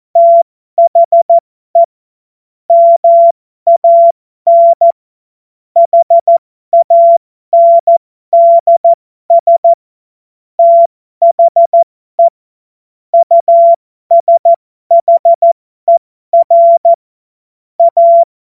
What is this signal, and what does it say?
Morse code, heard as a keyed tone, about 14 wpm, spaced as for 12 wpm, 680 Hz. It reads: THE MAN HANDS THE USHER A
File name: Morsekurs / Witze